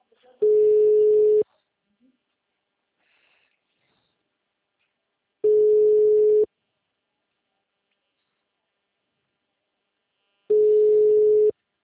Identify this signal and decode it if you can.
Ringback tone (Europe, 425 Hz), with speech.
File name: out- -2016- - - .19